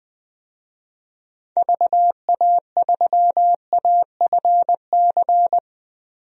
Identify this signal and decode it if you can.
Morse code, heard as a keyed tone, about 20 wpm, 700 Hz. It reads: VA3AFC
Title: DXCC-100 Call Signs - 6 Characters - 20wpm